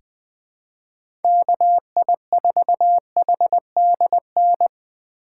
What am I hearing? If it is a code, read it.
Morse code, heard as a keyed tone, about 21 wpm, 700 Hz. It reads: KI4HDN
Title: USA Call Signs - 6 Characters - 8x Character Spacing - 20wpm